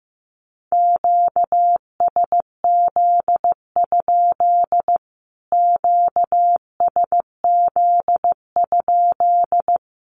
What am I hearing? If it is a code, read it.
Morse code, heard as a keyed tone, about 15 wpm, 700 Hz. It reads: QSZ? QSZ?
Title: Q-Codes Repeated 3x 15wpm